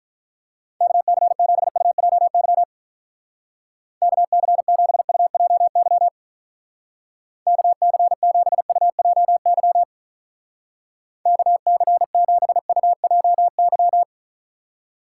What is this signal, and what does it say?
Morse code, heard as a keyed tone, about 45 wpm, 700 Hz. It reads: KC7UJY KC7UJY KC7UJY KC7UJY